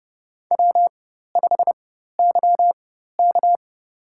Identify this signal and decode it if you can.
Morse code, heard as a keyed tone, about 30 wpm, spaced as for 14 wpm, 700 Hz. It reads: W5YK
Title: USA Call Signs - 4 Characters - 4x Character Spacing - 30wpm